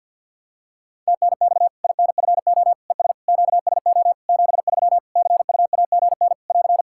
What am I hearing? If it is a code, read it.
Morse code, heard as a keyed tone, about 53 wpm, 700 Hz. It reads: TNX INFO ES QSO 73 CUAGN <AR>
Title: Practice QSO - Exchange 4 - Ending - 50wpm